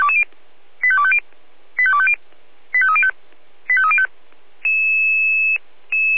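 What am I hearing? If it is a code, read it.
Five-tone selective call (ZVEI), heard as quick runs of five tones, tones of about 70 ms. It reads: 85250 85250 85305 85305
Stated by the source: ZVEI five-tone call 85101 received with an SDR.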